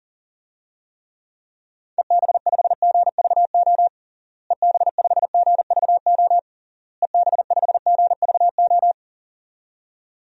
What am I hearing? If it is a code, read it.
Morse code, heard as a keyed tone, about 40 wpm, 700 Hz. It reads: EB5GVO EB5GVO EB5GVO
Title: DXCC-100 Call Signs - 6 Characters - Repeated 3x - 40wpm